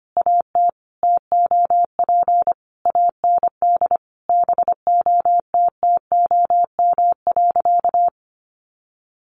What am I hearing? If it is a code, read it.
Morse code, heard as a keyed tone, about 25 wpm, 700 Hz. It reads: AT TOP AND BOTTOM.
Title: Sentences from Top 1400 Words 25wpm